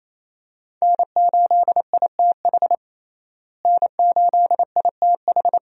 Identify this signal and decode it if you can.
Morse code, heard as a keyed tone, about 28 wpm, 700 Hz. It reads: N8ITH N8ITH
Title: USA Call Signs - 5 Characters - Repeated 3x - 28wpm